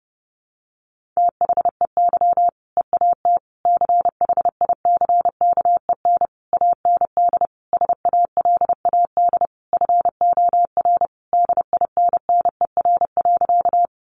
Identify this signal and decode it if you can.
Morse code, heard as a keyed tone, about 30 wpm, 700 Hz. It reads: THEY EAT CHICKEN AND SALAD FOR DINNER.
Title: Easy Phrases 30wpm